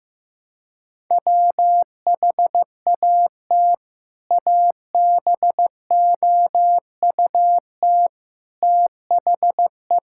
Morse code, heard as a keyed tone, about 15 wpm, 700 Hz. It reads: WHAT ABOUT THE
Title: Sentences from Top 1600 Words 15wpm